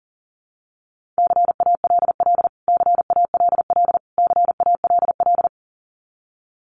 Morse code, heard as a keyed tone, about 40 wpm, 700 Hz. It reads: CALL CALL CALL
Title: Top 500 Words Repeated 40wpm